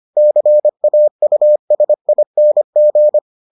Morse code, heard as a keyed tone, about 25 wpm, 600 Hz. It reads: CAUSING